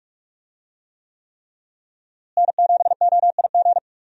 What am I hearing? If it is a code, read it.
Morse code, heard as a keyed tone, about 44 wpm, 700 Hz. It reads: N7OIG
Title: USA Call Signs - 5 Characters - Repeated 3x - 45wpm